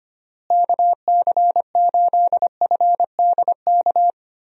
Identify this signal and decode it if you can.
Morse code, heard as a keyed tone, about 25 wpm, 700 Hz. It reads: KC8FDK
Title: USA Call Signs - 6 Characters - Repeated 3x - 25wpm